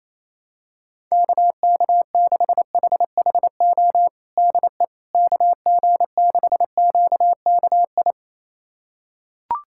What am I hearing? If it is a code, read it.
Morse code, heard as a keyed tone, about 28 wpm, 700 Hz. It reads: KK6HHO DE KG6QKI
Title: USA Callsign DE Callsign 28wpm